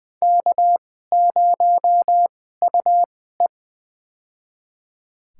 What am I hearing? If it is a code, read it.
Morse code, heard as a keyed tone, about 20 wpm, spaced as for 14 wpm, 700 Hz. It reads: K0UE
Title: USA Call Signs - 4 Characters - 2x Character Spacing - 20wpm